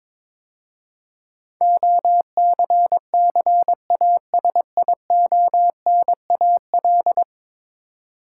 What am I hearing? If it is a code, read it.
Morse code, heard as a keyed tone, about 22 wpm, 700 Hz. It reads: OCCASIONAL
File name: practice / 22wpm / 10 Letter Words - First 3 Letters Given - 22wpm